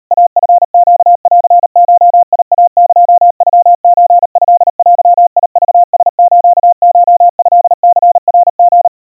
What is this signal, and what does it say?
Morse code (Japanese Wabun code), heard as a keyed tone, about 38 wpm, 700 Hz. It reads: イチネンゴイエノソトデクラスコトニナリ
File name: kawa / Morse sample / kamyu 140 cpm 700 Hz Crispy